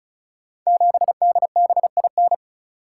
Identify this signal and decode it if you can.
Morse code, heard as a keyed tone, about 36 wpm, 700 Hz. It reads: 7DBIN
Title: Code Groups - 5 Letter:Numbers - 35wpm